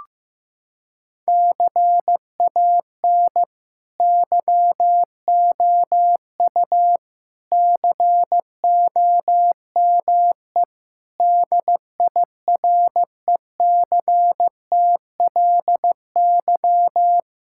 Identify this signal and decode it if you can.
Morse code, heard as a keyed tone, about 16 wpm, 700 Hz. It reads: CAN YOU COME DIRECTLY
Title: Sentences from Top 1000 Words - Review Part 5 - 15wpm